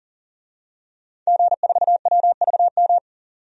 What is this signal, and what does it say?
Morse code, heard as a keyed tone, about 37 wpm, 700 Hz. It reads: G4WVM
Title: DXCC-100 Call Signs - 5 Characters - 40wpm